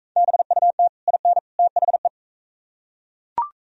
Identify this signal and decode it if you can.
Morse code, heard as a keyed tone, about 38 wpm, 700 Hz. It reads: BUT IN THE